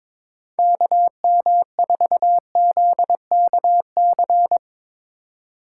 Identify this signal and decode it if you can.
Morse code, heard as a keyed tone, about 22 wpm, 700 Hz. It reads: KM4ZKC
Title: USA Call Signs - 2x Character Spacing - 22wpm